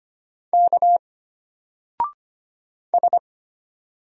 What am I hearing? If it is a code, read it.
Morse code, heard as a keyed tone, about 25 wpm, 700 Hz. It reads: K S
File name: Single-Letter-Number-25wpm